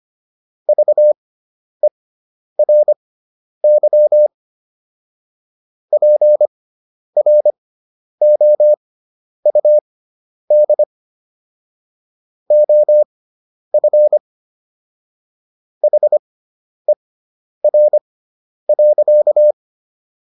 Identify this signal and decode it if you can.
Morse code, heard as a keyed tone, about 25 wpm, spaced as for 10 wpm, 600 Hz. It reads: VERY PROUD OF HER.